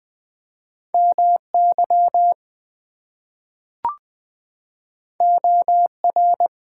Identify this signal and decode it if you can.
Morse code, heard as a keyed tone, about 20 wpm, 700 Hz. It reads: MY OR
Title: Two Letter Words - 20wpm